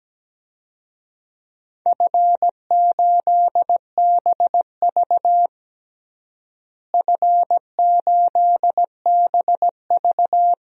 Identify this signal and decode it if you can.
Morse code, heard as a keyed tone, about 17 wpm, 700 Hz. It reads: F8BV F8BV
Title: DXCC-100 Call Signs - 4 Characters - Repeated 2x - 17wpm